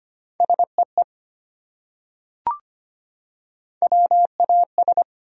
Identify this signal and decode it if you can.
Morse code, heard as a keyed tone, about 25 wpm, 700 Hz. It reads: SEE WAS
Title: Top 100 Words 25wpm